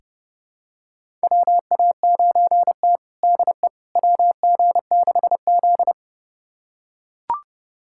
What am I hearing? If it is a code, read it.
Morse code, heard as a keyed tone, about 30 wpm, 700 Hz. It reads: WA9T DE WG6Z
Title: USA Callsign DE Callsign - 4 Characters - 30wpm